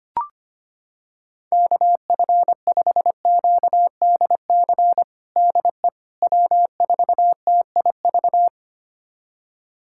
Morse code, heard as a keyed tone, about 25 wpm, 700 Hz. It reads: KF5QDC DE W4TIV
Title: USA Callsign DE Callsign 25wpm